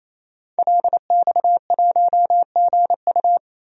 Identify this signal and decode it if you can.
Morse code, heard as a keyed tone, about 28 wpm, 700 Hz. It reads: LX1GU
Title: DXCC-100 Call Signs - 5 Characters - 28wpm